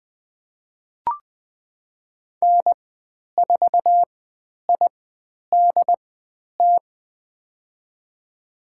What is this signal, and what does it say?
Morse code, heard as a keyed tone, about 20 wpm, spaced as for 10 wpm, 700 Hz. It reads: N4IDT
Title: 095 Lesson 16g - Letter c - Call Signs 20wpm